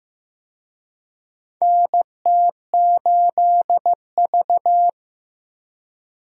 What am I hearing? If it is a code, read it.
Morse code, heard as a keyed tone, about 15 wpm, 700 Hz. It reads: NT8V
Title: USA Call Signs - 4 Characters - 15wpm